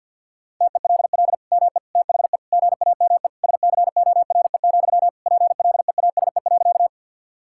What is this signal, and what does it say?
Morse code, heard as a keyed tone, about 50 wpm, 700 Hz. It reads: TELL ME THE GAME SCORE, PLEASE.
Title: Sentences from Top 1600 Words 50wpm